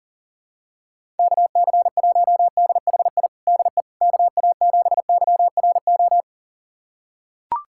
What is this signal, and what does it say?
Morse code, heard as a keyed tone, about 40 wpm, 700 Hz. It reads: KC1DHI DE KA7YRO
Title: USA Callsign DE Callsign - 6 Characters - 40wpm